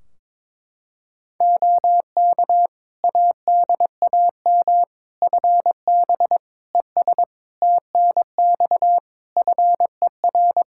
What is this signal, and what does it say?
Morse code, heard as a keyed tone, about 22 wpm, 700 Hz. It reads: OK ADAM FB ES TNX FER
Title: Practice QSO - Exchange 2 - Info - 22wpm